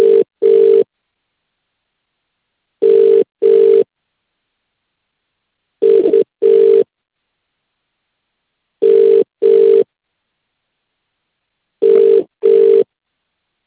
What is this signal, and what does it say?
Ringback tone (UK), 400 + 450 Hz.